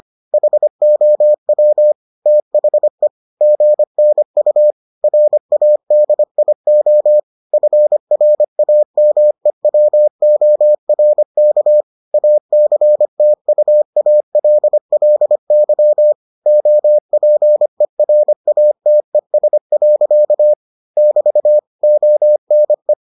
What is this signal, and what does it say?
Morse code, heard as a keyed tone, about 25 wpm, 600 Hz. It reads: HOW THE GNU RADIO FRAMEWORK ACTUALLY OPERATES. <BT> ONE